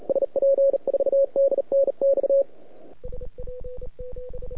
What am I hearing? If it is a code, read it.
Morse code, heard as a keyed tone, about 38 wpm, 550 Hz. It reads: SP4DNX
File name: rec contest / euday 2017 / SP3CW de SP7IVO PGA CW 3,5MHz.mp3